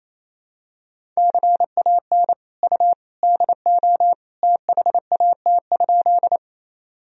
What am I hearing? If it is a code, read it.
Morse code, heard as a keyed tone, about 28 wpm, 700 Hz. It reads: CAN U DO THAT?
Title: Sentences from Top 100 Words and Abbreviations 28wpm